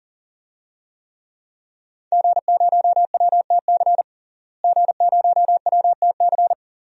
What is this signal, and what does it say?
Morse code, heard as a keyed tone, about 40 wpm, 700 Hz. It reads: G0WTC G0WTC
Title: DXCC-100 Call Signs - 5 Characters - Repeated 2x - 40wpm